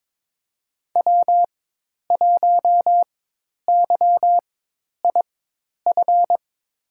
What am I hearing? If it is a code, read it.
Morse code, heard as a keyed tone, about 21 wpm, spaced as for 10 wpm, 700 Hz. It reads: W1YIF
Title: USA Call Signs - 5 Characters - 4x Character Spacing - 22wpm